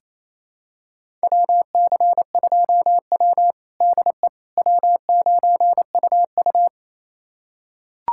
Morse code, heard as a keyed tone, about 28 wpm, 700 Hz. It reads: WC2W DE W9UU
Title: USA Callsign DE Callsign - 4 Characters - 28wpm